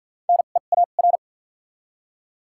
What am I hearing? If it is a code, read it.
Morse code, heard as a keyed tone, about 48 wpm, spaced as for 35 wpm, 700 Hz. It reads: NEAR
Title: 2x Character Spacing - Top 200 Words - 50wpm